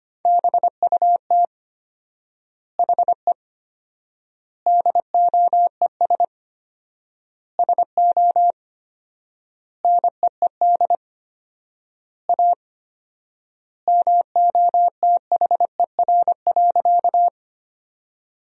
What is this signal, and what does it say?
Morse code, heard as a keyed tone, about 25 wpm, 700 Hz. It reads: BUT HE DOES SO NEED A MOTHER.
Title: 4x Word-Spacing - Sentences from Top 200 Words - 25wpm